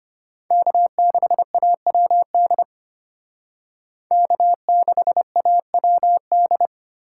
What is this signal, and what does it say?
Morse code, heard as a keyed tone, about 32 wpm, 700 Hz. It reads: K6AWD K6AWD